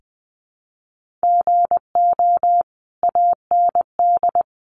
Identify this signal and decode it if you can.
Morse code, heard as a keyed tone, about 20 wpm, 700 Hz. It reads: GO AND
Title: Sentences from Top 1400 Words 20wpm